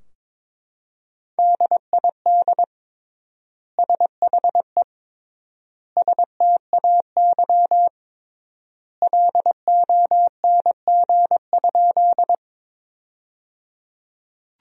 Morse code, heard as a keyed tone, about 22 wpm, 700 Hz. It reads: DID SHE STAY LONG?
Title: 3x Word-Spacing - Sentences from Top 500 Words - 22wpm